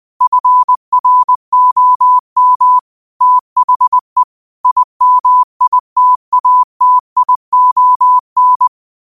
Morse code, heard as a keyed tone, about 20 wpm, 1000 Hz. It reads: FROM THE IMITATION
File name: qotdfiles / 20wpm